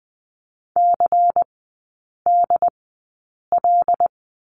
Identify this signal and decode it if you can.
Morse code, heard as a keyed tone, about 20 wpm, spaced as for 8 wpm, 700 Hz. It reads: CDL
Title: 218 Lesson 35e - Number 0 - Three Characters 20wpm